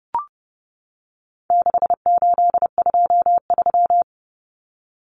Morse code, heard as a keyed tone, about 30 wpm, 700 Hz. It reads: 6823
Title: Numbers - 4-Digit 30wpm